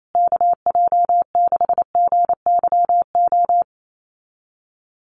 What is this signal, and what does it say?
Morse code, heard as a keyed tone, about 28 wpm, 700 Hz. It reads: KJ6GYO
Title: USA Call Signs Repeated 2x 28wpm